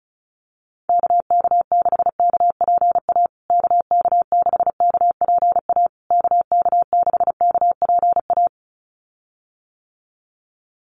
Morse code, heard as a keyed tone, about 34 wpm, 700 Hz. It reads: KK6KPA KK6KPA KK6KPA
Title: USA Call Signs Repeated 3x 35wpm